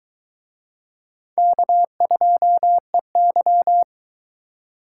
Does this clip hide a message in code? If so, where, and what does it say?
Yes: Morse code from 1.4 to 3.8 s, heard as a keyed tone, about 22 wpm, 700 Hz. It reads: K2EY